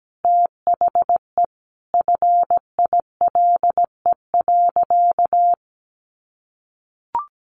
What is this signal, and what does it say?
Morse code, heard as a keyed tone, about 17 wpm, 700 Hz. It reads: THE FILE.
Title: Sentences from Top 1200 Words 17wpm